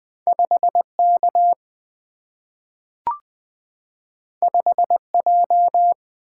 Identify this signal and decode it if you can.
Morse code, heard as a keyed tone, about 20 wpm, 700 Hz. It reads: 5 K 5 J
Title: Single-Letter Single-Number 20wpm